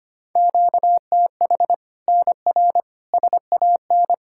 Morse code, heard as a keyed tone, about 25 wpm, 700 Hz. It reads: QTH NR SAN
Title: Practice QSO - Exchange 1 - Beginning - 25wpm